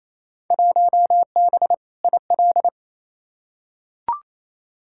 Morse code, heard as a keyed tone, about 28 wpm, 700 Hz. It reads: 1B IL
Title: Field Day Exchanges Encore - 28wpm